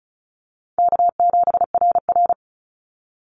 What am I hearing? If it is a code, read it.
Morse code, heard as a keyed tone, about 34 wpm, 700 Hz. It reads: K7RR
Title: Easy Call Signs Repeated 2x - 35wpm